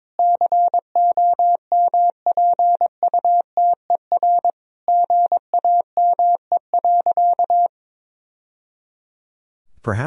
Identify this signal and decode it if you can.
Morse code, heard as a keyed tone, about 22 wpm, 700 Hz. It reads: COMPUTER GAME.
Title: Sentences from Top 1000 Words - Review Part 3 - 22wpm